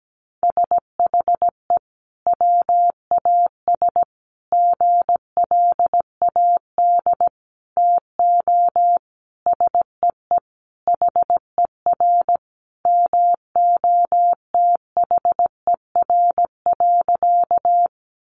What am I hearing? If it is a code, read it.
Morse code, heard as a keyed tone, about 17 wpm, 700 Hz. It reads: SHE WAS GLAD TO SEE HER MOTHER.
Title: Sentences from Top 800 Words 17wpm